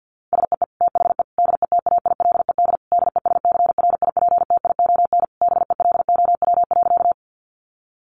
Morse code, heard as a keyed tone, about 50 wpm, 700 Hz. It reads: SEE THE DETAILED DESCRIPTION BELOW.
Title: Sentences from Top 1500 Words 50wpm